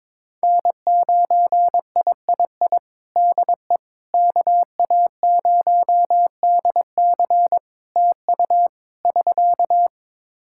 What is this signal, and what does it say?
Morse code, heard as a keyed tone, about 22 wpm, 700 Hz. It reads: N9III DE KA0DC TU <SK>
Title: Practice QSO - Exchange 4 - Ending - 22wpm